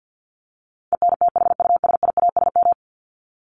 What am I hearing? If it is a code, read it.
Morse code, heard as a keyed tone, about 50 wpm, 700 Hz. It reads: ENTHUSIASM
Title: Top 1751-2000 Words 50wpm